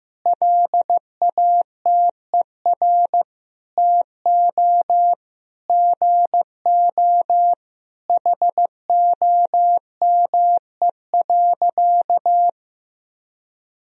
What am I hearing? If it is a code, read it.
Morse code, heard as a keyed tone, about 15 wpm, 700 Hz. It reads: LATER TO GO HOME.
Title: Sentences from Top 1000 Words - Review Part 4 - 15wpm